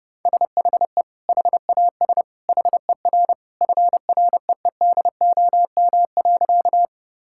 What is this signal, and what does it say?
Morse code, heard as a keyed tone, about 30 wpm, 700 Hz. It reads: SHE HAS HER FREEDOM.
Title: Sentences from Top 1000 Words - Review Part 5 - 30wpm